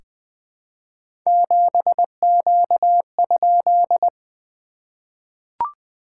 Morse code, heard as a keyed tone, about 20 wpm, 690 Hz. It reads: 7Q?
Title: SOTA QSO Practice 20wpm